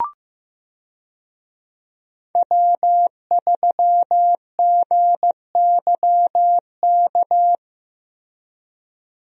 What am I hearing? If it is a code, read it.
Morse code, heard as a keyed tone, about 15 wpm, 700 Hz. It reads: W3GYK
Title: USA Call Signs - 5 Characters - Repeated 3x - 15wpm